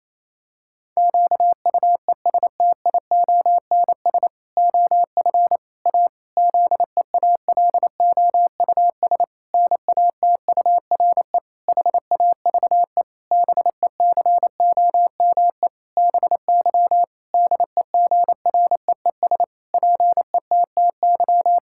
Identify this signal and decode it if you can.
Morse code, heard as a keyed tone, about 28 wpm, 700 Hz. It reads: QUESTIONS OF A ZEALOUS NATURE HAVE BECOME BY DEGREES PETTY
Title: Pangram Challenge v2 - 28wpm